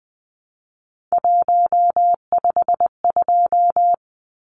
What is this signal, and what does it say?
Morse code, heard as a keyed tone, about 20 wpm, 700 Hz. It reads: 152
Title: Numbers - 3-Digit 20wpm